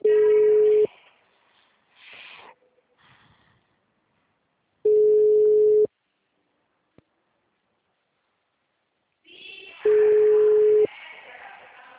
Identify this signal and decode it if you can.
Ringback tone (Europe, 425 Hz).
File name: out- -2002- - - .6141